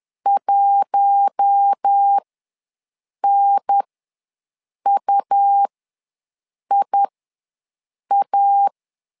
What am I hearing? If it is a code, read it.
Morse code, heard as a keyed tone, about 10 wpm, spaced as for 6 wpm, 800 Hz. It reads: 1NUIA
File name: LEZIONI / 4 Q1 ws 10wpm 2538